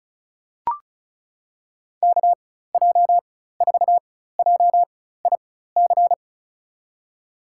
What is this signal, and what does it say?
Morse code, heard as a keyed tone, about 36 wpm, spaced as for 16 wpm, 700 Hz. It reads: KJ4JIC
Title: USA Call Signs - 6 Characters - 4x Character Spacing - 35wpm